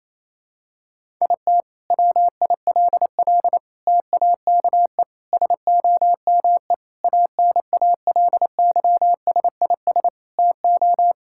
Morse code, heard as a keyed tone, about 28 wpm, 700 Hz. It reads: IT WILL TAKE SOME ANALYSIS TO
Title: Sentences from Top 1000 Words - Review Part 5 - 28wpm